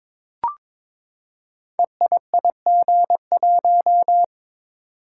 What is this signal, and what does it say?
Morse code, heard as a keyed tone, about 22 wpm, 700 Hz. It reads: EIIG1
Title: Code Groups - 5 Letter:Numbers - 22wpm